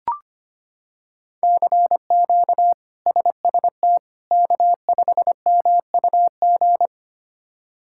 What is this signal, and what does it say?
Morse code, heard as a keyed tone, about 25 wpm, 700 Hz. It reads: CQ SST K5MUG
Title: K1USN SST QSO Practice - 25wpm